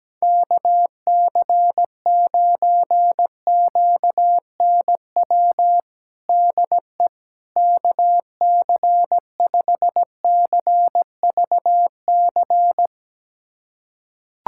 Morse code, heard as a keyed tone, about 17 wpm, 700 Hz. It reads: KC9QNW DE KC5CVC
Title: USA Callsign DE Callsign 17wpm